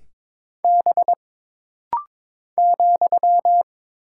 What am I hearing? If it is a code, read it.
Morse code, heard as a keyed tone, about 22 wpm, 700 Hz. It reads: B ,
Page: This rapid-fire timing is geared to push your ICR (Instant Character Recognition) skills.